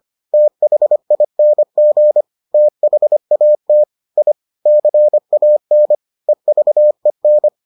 Morse code, heard as a keyed tone, about 25 wpm, 600 Hz. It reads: THING THAT I CAN EVEN